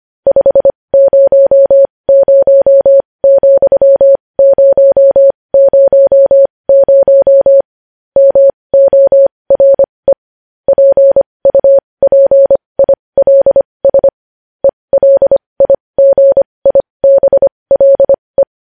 Morse code, heard as a keyed tone, about 25 wpm, spaced as for 20 wpm, 550 Hz. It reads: 500,000 MORE PUPILS ELIGIBLE
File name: podcast / News headlines-1x-25x20